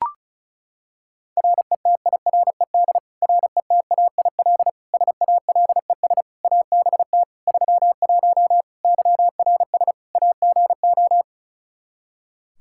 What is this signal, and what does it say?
Morse code, heard as a keyed tone, about 36 wpm, 700 Hz. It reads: RETIRED RETAIL SALES ABT 31 YRS AGO
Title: Practice QSO - Exchange 3 - More Info - 35wpm